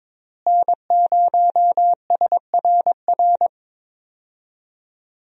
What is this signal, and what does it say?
Morse code, heard as a keyed tone, about 22 wpm, 700 Hz. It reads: N0SRR
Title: USA Call Signs - 5 Characters - Repeated 2x - 22wpm